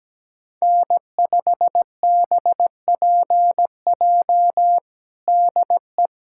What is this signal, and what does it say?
Morse code, heard as a keyed tone, about 17 wpm, 700 Hz. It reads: N5BPJ DE
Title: K1USN SST QSO Practice 17wpm